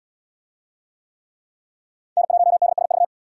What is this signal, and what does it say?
Morse code, heard as a keyed tone, about 74 wpm, 700 Hz. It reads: N1DSV